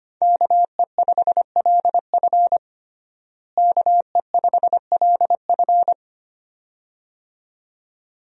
Morse code, heard as a keyed tone, about 25 wpm, 700 Hz. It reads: KE5LF KE5LF
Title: USA Call Signs - 5 Characters - Repeated 2x - 25wpm